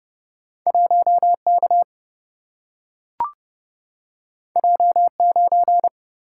Morse code, heard as a keyed tone, about 30 wpm, 700 Hz. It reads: 1 K J 9
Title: Single-Letter Single-Number 30wpm